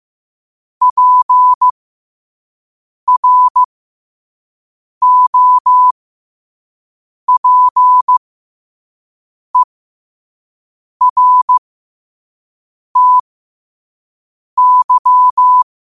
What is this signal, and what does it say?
Morse code, heard as a keyed tone, about 15 wpm, spaced as for 5 wpm, 1000 Hz. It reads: PROPERTY
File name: 5wpm